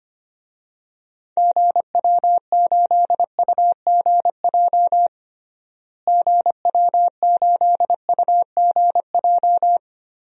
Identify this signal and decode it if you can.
Morse code, heard as a keyed tone, about 25 wpm, 700 Hz. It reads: GW8UGJ GW8UGJ
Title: DXCC-100 Call Signs - 6 Characters - Repeated 3x - 25wpm